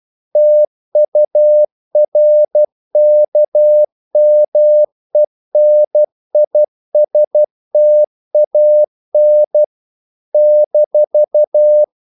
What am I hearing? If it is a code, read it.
Morse code, heard as a keyed tone, about 12 wpm, 600 Hz. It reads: TURKMENISTAN -
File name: CW / Practice / NationalCapitals12